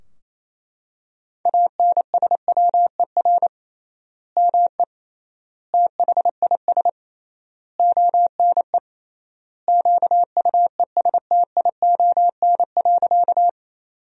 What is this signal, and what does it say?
Morse code, heard as a keyed tone, about 28 wpm, 700 Hz. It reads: ANSWER ME THIS ONE QUESTION.
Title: 3x Word-Spacing - Sentences from Top 400 Words - 28wpm